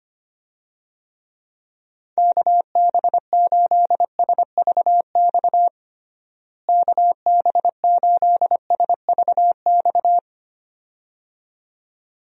Morse code, heard as a keyed tone, about 25 wpm, 700 Hz. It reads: KB8SVX KB8SVX
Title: USA Call Signs - 6 Characters - Repeated 2x - 25wpm